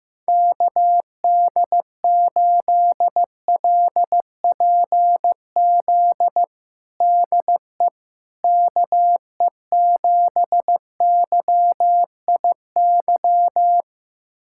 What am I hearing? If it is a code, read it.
Morse code, heard as a keyed tone, about 15 wpm, 700 Hz. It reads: KD8LPZ DE KE7YIY
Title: USA Callsign DE Callsign - 6 Characters - 15wpm